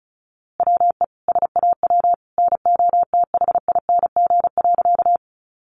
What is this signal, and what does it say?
Morse code, heard as a keyed tone, about 34 wpm, 700 Hz. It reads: WE SAW NOTHING.
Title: A Sound Of Thunder 35wpm